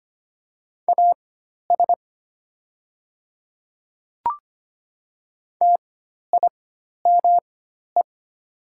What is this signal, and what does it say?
Morse code, heard as a keyed tone, about 25 wpm, spaced as for 12 wpm, 700 Hz. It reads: AS TIME
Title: 4x Character Spacing - Top 100 Words - 25wpm